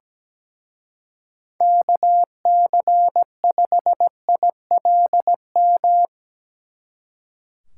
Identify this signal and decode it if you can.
Morse code, heard as a keyed tone, about 19 wpm, 700 Hz. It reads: KC5ILM
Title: Speed Racing - USA 2x3 Call Signs 17wpm